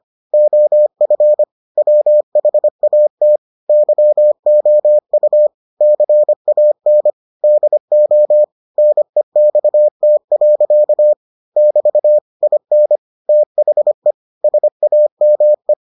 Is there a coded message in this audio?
Yes: Morse code from 0.3 to 15.7 s, heard as a keyed tone, about 25 wpm, 600 Hz. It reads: OF WHAT YOU CAN DO NEXT. <BT> IN THE SAME